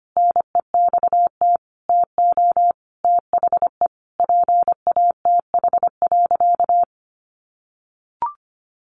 Morse code, heard as a keyed tone, about 25 wpm, 700 Hz. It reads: NEXT TO THE PATH.
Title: A Sound Of Thunder 25wpm